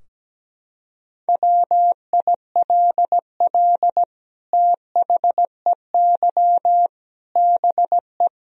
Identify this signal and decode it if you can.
Morse code, heard as a keyed tone, about 17 wpm, 700 Hz. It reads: WILL THEY BE
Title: Sentences from Top 1000 Words - Review Part 4 - 17wpm